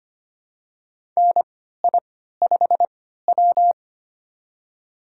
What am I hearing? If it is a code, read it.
Morse code, heard as a keyed tone, about 25 wpm, spaced as for 14 wpm, 700 Hz. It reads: NI5W
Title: USA Call Signs - 4 Characters - 3x Character Spacing - 25wpm